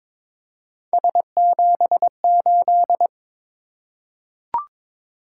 Morse code, heard as a keyed tone, about 22 wpm, 700 Hz. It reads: S78
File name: practice / 22wpm / Three Letter-Number Combinations - 22wpm